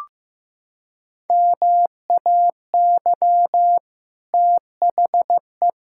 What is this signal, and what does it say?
Morse code, heard as a keyed tone, about 15 wpm, 700 Hz. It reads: MAY THE
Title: Sentences from Top 1600 Words 15wpm